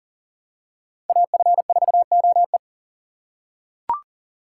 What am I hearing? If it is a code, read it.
Morse code, heard as a keyed tone, about 40 wpm, 700 Hz. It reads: AF4OE
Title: USA Call Signs - 5 Characters - 2x Character Spacing - 40wpm